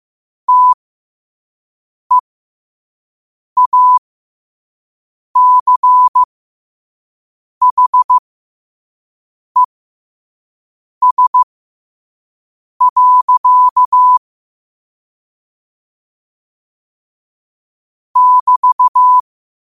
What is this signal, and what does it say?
Morse code, heard as a keyed tone, about 15 wpm, spaced as for 5 wpm, 1000 Hz. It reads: TEACHES. <BT>
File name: qotdfiles / 5wpm